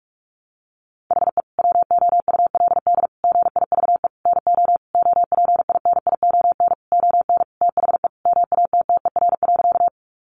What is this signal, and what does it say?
Morse code, heard as a keyed tone, about 44 wpm, 700 Hz. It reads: HE WOULD GIVE NO OPINION ON THE MATTER.
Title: Sentences from Top 1200 Words 45wpm